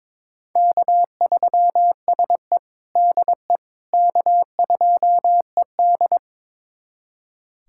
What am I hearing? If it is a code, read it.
Morse code, heard as a keyed tone, about 22 wpm, 700 Hz. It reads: K3SE DE K2ED
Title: USA Callsign DE Callsign - 4 Characters - 22wpm